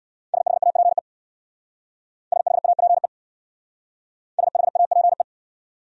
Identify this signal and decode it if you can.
Morse code, heard as a keyed tone, about 66 wpm, 700 Hz. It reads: SHAPE SHAPE SHAPE